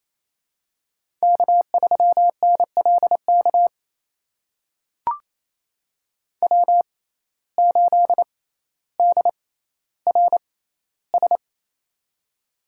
Morse code, heard as a keyed tone, about 28 wpm, spaced as for 14 wpm, 700 Hz. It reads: K3NLK W8DRS
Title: USA Call Signs - 5 Characters - 6x Character Spacing - 28wpm